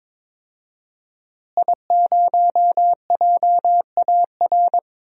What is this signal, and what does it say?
Morse code, heard as a keyed tone, about 23 wpm, 700 Hz. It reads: I0JAR
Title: DXCC-100 Call Signs - 5 Characters - Repeated 3x - 22wpm